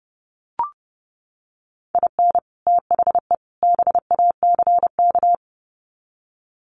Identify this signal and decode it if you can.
Morse code, heard as a keyed tone, about 30 wpm, 700 Hz. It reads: IN THE BACK
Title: Sets of 3 Words 30wpm